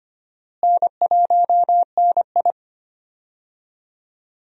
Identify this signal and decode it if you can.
Morse code, heard as a keyed tone, about 25 wpm, 700 Hz. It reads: N1NI
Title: USA Call Signs - 4 Characters - Repeated 2x - 25wpm